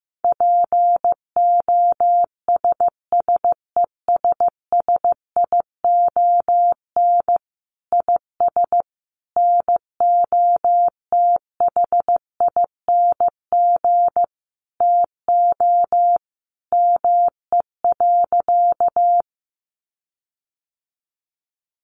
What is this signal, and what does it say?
Morse code, heard as a keyed tone, about 14 wpm, 700 Hz. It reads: POSSESSION IS NOTHING TO ME.
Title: Sentences from Top 1400 Words 15wpm